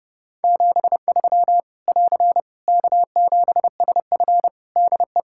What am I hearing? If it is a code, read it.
Morse code, heard as a keyed tone, about 30 wpm, 700 Hz. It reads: 73 <AR> K7SF DE
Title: K1USN SST QSO Practice 30wpm